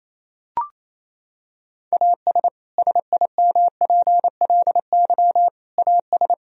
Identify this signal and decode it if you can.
Morse code, heard as a keyed tone, about 28 wpm, 700 Hz. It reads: AS SIMPLY AS
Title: Sentences from Top 1000 Words - Review Part 4 - 28wpm